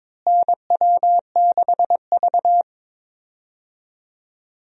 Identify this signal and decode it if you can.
Morse code, heard as a keyed tone, about 21 wpm, 700 Hz. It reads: NW6V
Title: USA Call Signs - 4 Characters - Repeated 3x - 22wpm